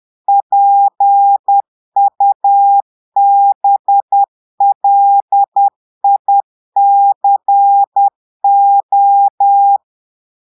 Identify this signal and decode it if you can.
Morse code, heard as a keyed tone, about 10 wpm, 800 Hz. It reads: PUBLICO
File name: portugais texte en clair 2 vitesse 10 mots minutes